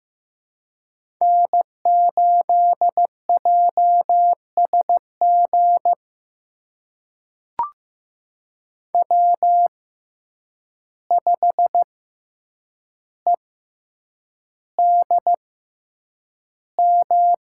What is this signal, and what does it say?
Morse code, heard as a keyed tone, about 15 wpm, spaced as for 8 wpm, 700 Hz. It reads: N8JSG W5EDM
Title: USA Call Signs - 5 Characters - 6x Character Spacing - 15wpm